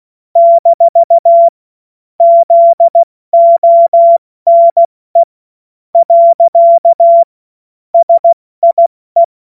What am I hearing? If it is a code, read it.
Morse code, heard as a keyed tone, about 16 wpm, spaced as for 13 wpm, 680 Hz. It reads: - ZONE. SIE
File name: Witz-30-September-2025-70-F80